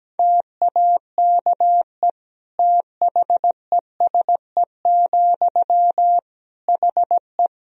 Morse code, heard as a keyed tone, about 17 wpm, 700 Hz. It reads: TAKE THESE, HE
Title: Sentences from Top 300 Words and Abbreviations 17wpm